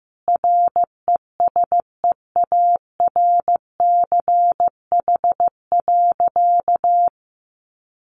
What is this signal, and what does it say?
Morse code, heard as a keyed tone, about 15 wpm, 700 Hz. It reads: RESEARCH.
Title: Sentences from Top 1400 Words 15wpm